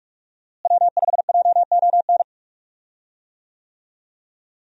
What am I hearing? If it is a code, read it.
Morse code, heard as a keyed tone, about 41 wpm, 700 Hz. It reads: W5JON
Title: USA Call Signs - 5 Characters - Repeated 2x - 45wpm